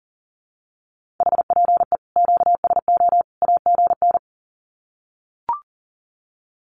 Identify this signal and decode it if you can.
Morse code, heard as a keyed tone, about 40 wpm, 700 Hz. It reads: HPE QSO AGN
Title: Common QSO Phrases 40wpm